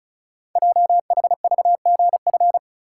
Morse code, heard as a keyed tone, about 36 wpm, 700 Hz. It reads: JHVGF
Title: Code Groups - 5 Letters - 35wpm